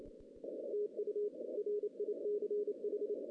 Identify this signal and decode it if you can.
Morse code, heard as a keyed tone, about 31 wpm, 440 Hz. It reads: TU RICH
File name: 2016 JAN CW NAQP / 2016 Jan naqp k5ot - 0-8